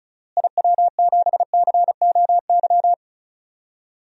Morse code, heard as a keyed tone, about 34 wpm, 700 Hz. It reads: IW7COY
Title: DXCC-100 Call Signs - 6 Characters - Repeated 2x - 35wpm